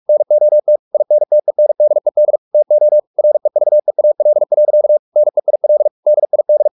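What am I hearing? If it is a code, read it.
Morse code, heard as a keyed tone, about 44 wpm, 600 Hz. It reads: NOT INTENDED TO REVEAL. NEIL DID
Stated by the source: Morse practice at 45 wpm, the novel Never by Ken Follett in English, from CW med Gnister.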